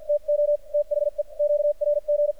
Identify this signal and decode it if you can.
Morse code, heard as a keyed tone, about 49 wpm, spaced as for 43 wpm, 590 Hz. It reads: TO THE QRG
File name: archived / CRSnet